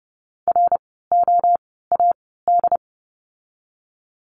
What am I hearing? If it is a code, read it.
Morse code, heard as a keyed tone, about 30 wpm, spaced as for 17 wpm, 700 Hz. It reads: ROAD
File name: Lesson 16f - Letter c - Words 30wpm